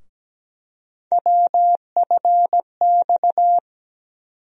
Morse code, heard as a keyed tone, about 17 wpm, 700 Hz. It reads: WFX
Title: USA Call Sign Postfixes - 3 Letters 17wpm